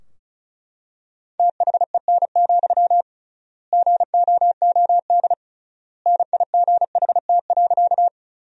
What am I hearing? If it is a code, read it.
Morse code, heard as a keyed tone, about 36 wpm, 700 Hz. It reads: THEN, GOOD NIGHT.
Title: 3x Word-Spacing - Sentences from Top 300 Words - 35wpm